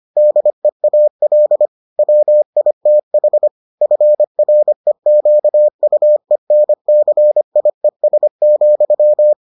Morse code, heard as a keyed tone, about 25 wpm, 600 Hz. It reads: DEAL WITH FREQUENCIES,